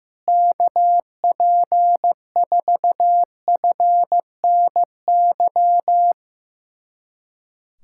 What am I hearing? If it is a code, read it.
Morse code, heard as a keyed tone, about 15 wpm, 700 Hz. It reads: KP4FNY
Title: USA Callsign DE Callsign 15wpm